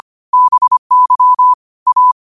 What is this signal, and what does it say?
Morse code, heard as a keyed tone, about 25 wpm, 1000 Hz. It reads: BY A